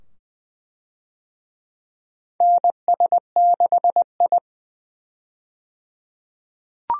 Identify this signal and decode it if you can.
Morse code, heard as a keyed tone, about 20 wpm, 700 Hz. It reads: NS6I
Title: USA Call Signs - 4 Characters - 20wpm